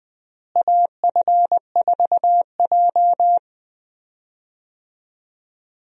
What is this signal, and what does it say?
Morse code, heard as a keyed tone, about 20 wpm, 700 Hz. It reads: AF4J
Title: USA Call Signs - 4 Characters - Repeated 2x - 20wpm